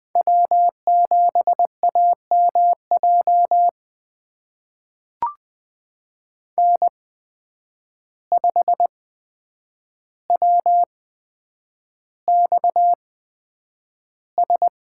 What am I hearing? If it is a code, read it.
Morse code, heard as a keyed tone, about 20 wpm, spaced as for 9 wpm, 700 Hz. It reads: W7AMJ N5WXS
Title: USA Call Signs - 5 Characters - 8x Character Spacing - 20wpm